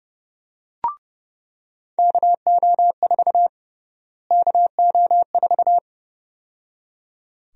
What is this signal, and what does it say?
Morse code, heard as a keyed tone, about 31 wpm, 700 Hz. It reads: KO4 KO4
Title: USA Call Sign Prefixes - Repeated 2x - 30wpm